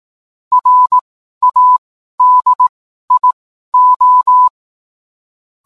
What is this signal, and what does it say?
Morse code, heard as a keyed tone, about 18 wpm, spaced as for 12 wpm, 1000 Hz. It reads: RADIO